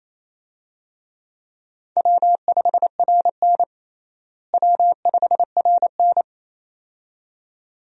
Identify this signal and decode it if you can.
Morse code, heard as a keyed tone, about 28 wpm, 700 Hz. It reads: W5RN W5RN
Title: USA Call Signs - 4 Characters - Repeated 2x - 28wpm